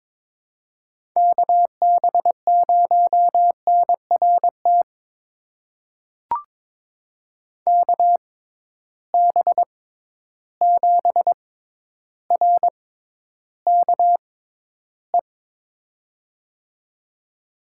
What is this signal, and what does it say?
Morse code, heard as a keyed tone, about 22 wpm, spaced as for 11 wpm, 700 Hz. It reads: KB0NRT KB7RKE
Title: USA Call Signs - 6 Characters - 6x Character Spacing - 22wpm